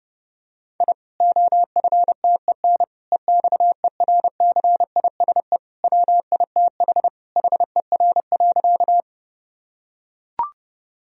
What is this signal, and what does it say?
Morse code, heard as a keyed tone, about 32 wpm, 700 Hz. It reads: I OFTEN EXERCISE WITH HER.
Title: Sentences from Top 1000 Words - Review Part 4 - 30wpm